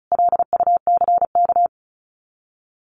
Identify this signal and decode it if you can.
Morse code, heard as a keyed tone, about 34 wpm, 700 Hz. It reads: LUCK
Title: Common QSO Phrases 35wpm